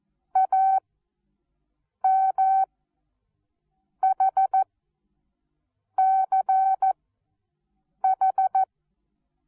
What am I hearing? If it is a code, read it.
Morse code, heard as a keyed tone, about 14 wpm, spaced as for 6 wpm, 770 Hz. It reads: AMHCH
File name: Onlinelehrgang / morsen / DJ4UF Morsen19